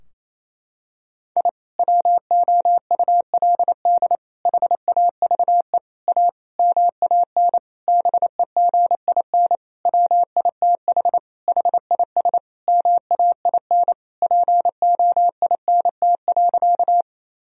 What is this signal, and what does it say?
Morse code, heard as a keyed tone, about 28 wpm, 700 Hz. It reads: I WOULD HAVE A MAN BEGIN WITH HIS MAIN POINT.
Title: Sentences from Top 1000 Words - Review Part 2 - 28wpm